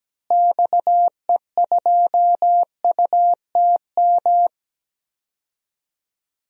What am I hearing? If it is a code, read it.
Morse code, heard as a keyed tone, about 17 wpm, 700 Hz. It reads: XE2UTM
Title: DXCC-100 Call Signs - 6 Characters - Repeated 2x - 17wpm